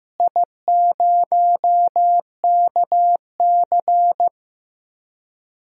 Morse code, heard as a keyed tone, about 15 wpm, 700 Hz. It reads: I0KC
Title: DXCC-100 Call Signs - 4 Characters - Repeated 2x - 15wpm